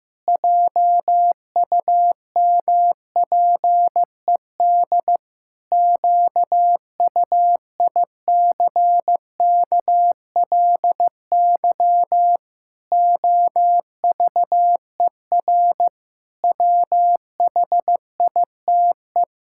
Morse code, heard as a keyed tone, about 15 wpm, 700 Hz. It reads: JUMPED QUICKLY OVER WHITE
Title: Pangram Challenge v2 - 15wpm